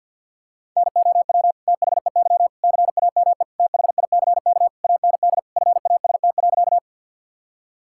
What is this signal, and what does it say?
Morse code, heard as a keyed tone, about 50 wpm, 700 Hz. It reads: NOW THEY CAME THICK AND FAST.
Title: Sentences from Top 1600 Words 50wpm